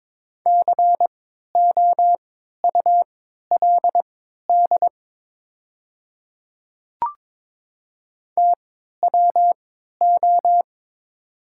Morse code, heard as a keyed tone, about 22 wpm, spaced as for 13 wpm, 690 Hz. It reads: COULD TWO
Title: 3x Character Spacing - Top 200 Words - 22wpm